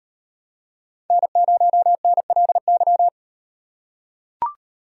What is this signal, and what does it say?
Morse code, heard as a keyed tone, about 38 wpm, 700 Hz. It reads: N0NLY